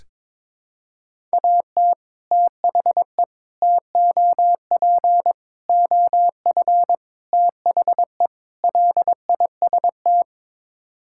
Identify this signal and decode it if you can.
Morse code, heard as a keyed tone, about 22 wpm, 700 Hz. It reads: AT THE TOP OF THE LIST
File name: practice / 22wpm / Sets of 6 Words - 22wpm